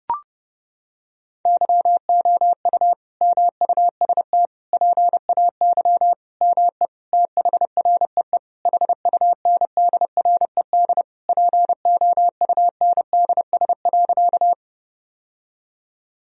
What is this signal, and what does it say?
Morse code, heard as a keyed tone, about 30 wpm, 700 Hz. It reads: YOU MUST PAY ME THREE HUNDRED POUNDS.
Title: Sentences from Top 1000 Words - Review Part 2 - 30wpm